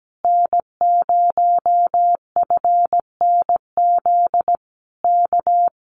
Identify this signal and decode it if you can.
Morse code, heard as a keyed tone, about 17 wpm, 700 Hz. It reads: N0FNZ K
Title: Common QSO Phrases Encore 17wpm